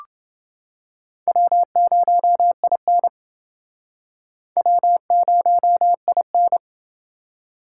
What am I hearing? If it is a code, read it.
Morse code, heard as a keyed tone, about 28 wpm, 700 Hz. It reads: W0IN W0IN